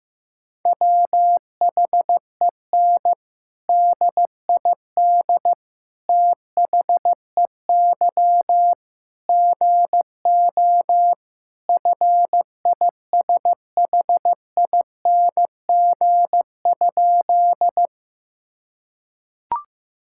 Morse code, heard as a keyed tone, about 16 wpm, 700 Hz. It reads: WHEN DID THEY GO FISHING?
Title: Sentences from Top 1000 Words - Review Part 5 - 15wpm